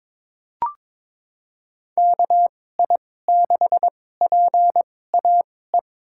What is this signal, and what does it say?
Morse code, heard as a keyed tone, about 23 wpm, spaced as for 16 wpm, 700 Hz. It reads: KI6PAE
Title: USA Call Signs - 6 Characters - 2x Character Spacing - 22wpm